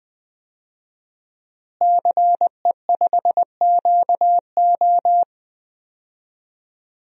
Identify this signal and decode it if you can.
Morse code, heard as a keyed tone, about 21 wpm, 700 Hz. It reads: CE5QO
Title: DXCC-100 Call Signs - 5 Characters - Repeated 2x - 20wpm